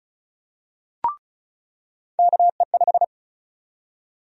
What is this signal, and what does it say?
Morse code, heard as a keyed tone, about 34 wpm, 700 Hz. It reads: KE5
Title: USA Call Sign Prefixes - 35wpm